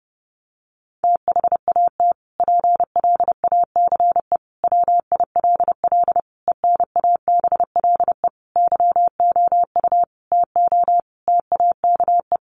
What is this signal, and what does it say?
Morse code, heard as a keyed tone, about 30 wpm, 700 Hz. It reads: THAT PLACE WILL ENABLE YOU TO TAKE
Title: Sentences from Top 1500 Words 30wpm